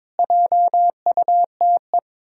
Morse code, heard as a keyed tone, about 22 wpm, 700 Hz. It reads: JUTE
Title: Pangram Challenge v2 - 22wpm